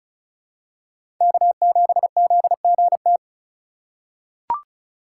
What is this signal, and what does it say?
Morse code, heard as a keyed tone, about 34 wpm, 700 Hz. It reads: K7ZGT
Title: USA Call Signs - 5 Characters - 4x Character Spacing - 35wpm